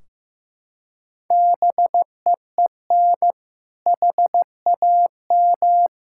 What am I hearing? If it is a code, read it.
Morse code, heard as a keyed tone, about 16 wpm, 700 Hz. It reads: BEEN HAM
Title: Practice QSO - Exchange 3 - More Info - 15wpm